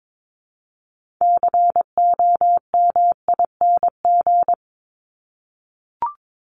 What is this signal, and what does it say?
Morse code, heard as a keyed tone, about 22 wpm, 700 Hz. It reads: COMING
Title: ING Suffix - Top 100 Words with Base Words 22wpm